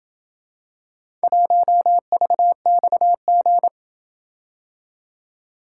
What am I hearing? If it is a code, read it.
Morse code, heard as a keyed tone, about 28 wpm, 700 Hz. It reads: 1VXG